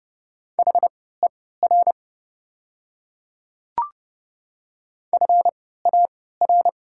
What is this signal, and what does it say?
Morse code, heard as a keyed tone, about 30 wpm, spaced as for 17 wpm, 700 Hz. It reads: HER FAR
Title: 3x Character Spacing - Top 500 Words - 30wpm